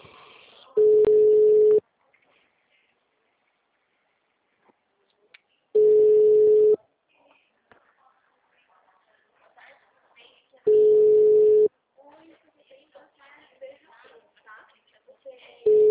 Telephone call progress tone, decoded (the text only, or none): ringback tone